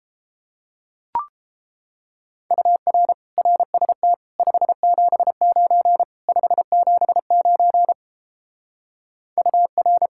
Morse code, heard as a keyed tone, about 31 wpm, 700 Hz. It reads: UR RST 579 579 UR